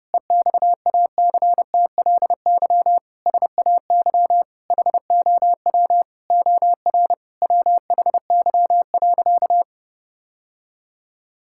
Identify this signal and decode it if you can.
Morse code, heard as a keyed tone, about 30 wpm, 700 Hz. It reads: EXACTLY SAY HOW OR WHY.
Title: Sentences from Top 1000 Words - Review Part 6 - 30wpm